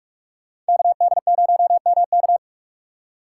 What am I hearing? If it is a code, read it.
Morse code, heard as a keyed tone, about 46 wpm, 700 Hz. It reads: KD0MK
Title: USA Call Signs - 5 Characters - 6x Character Spacing - 45wpm